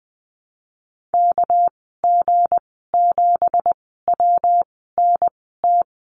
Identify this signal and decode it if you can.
Morse code, heard as a keyed tone, about 20 wpm, spaced as for 14 wpm, 700 Hz. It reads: KG7WNT
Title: 231 Lesson 37g - Letter x - Call Signs 20wpm